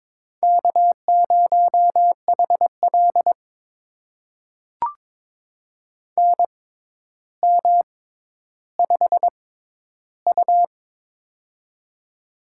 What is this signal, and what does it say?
Morse code, heard as a keyed tone, about 22 wpm, spaced as for 11 wpm, 700 Hz. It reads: K0HL NM5U
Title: USA Call Signs - 4 Characters - 6x Character Spacing - 22wpm